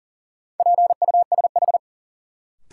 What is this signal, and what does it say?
Morse code, heard as a keyed tone, about 40 wpm, 700 Hz. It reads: PUSH
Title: Sentences from Top 1300 Words - New Word Drill - 40wpm